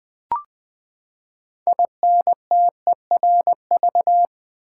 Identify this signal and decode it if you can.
Morse code, heard as a keyed tone, about 20 wpm, 700 Hz. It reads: INTERV
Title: Build A Word - 6 Letters At A Time - 12-Letter Words - 20wpm